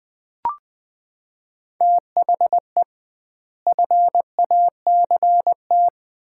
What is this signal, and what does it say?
Morse code, heard as a keyed tone, about 20 wpm, 700 Hz. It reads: THE FACT
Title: 2x Word-Spacing - Sets of 3 Words - 20wpm